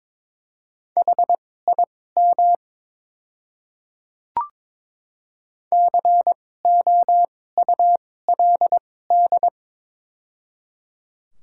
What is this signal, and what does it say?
Morse code, heard as a keyed tone, about 21 wpm, spaced as for 16 wpm, 700 Hz. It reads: HIM COULD
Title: 2x Character Spacing - Top 300 Words - 22wpm